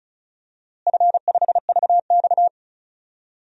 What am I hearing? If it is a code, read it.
Morse code, heard as a keyed tone, about 34 wpm, 700 Hz. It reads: F5VX
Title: DXCC-100 Call Signs - 4 Characters - Repeated 2x - 35wpm